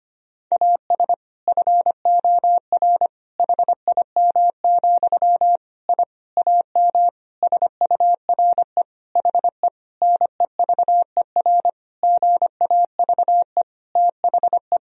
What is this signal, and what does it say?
Morse code, heard as a keyed tone, about 25 wpm, 700 Hz. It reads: AS FOR HIM, I AM SURE HE NEVER GAVE THE
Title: Sentences from Top 1000 Words - Review Part 2 - 25wpm